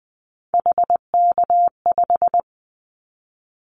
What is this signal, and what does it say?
Morse code, heard as a keyed tone, about 20 wpm, 700 Hz. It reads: HK5
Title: Callsign Prefixes for Top 100 DXCC - 20wpm